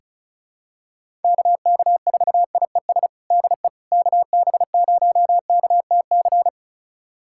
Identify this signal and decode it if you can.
Morse code, heard as a keyed tone, about 34 wpm, 700 Hz. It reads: KK4IES DE KB0KTC
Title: USA Callsign DE Callsign 35wpm